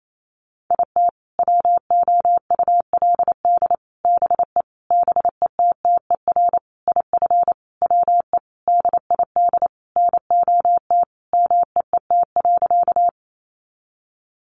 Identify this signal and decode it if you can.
Morse code, heard as a keyed tone, about 28 wpm, 700 Hz. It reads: IT WOULD BE BETTER IF WE DID NOT MEET.
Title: Sentences from Top 1100 Words 28wpm